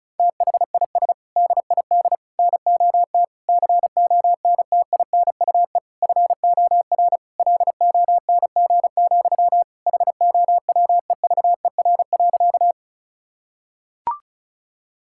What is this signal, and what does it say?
Morse code, heard as a keyed tone, about 34 wpm, 700 Hz. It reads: THIS DID NOT CONTINUE FOR LONG, HOWEVER.
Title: Sentences from Top 1000 Words - Review Part 6 - 35wpm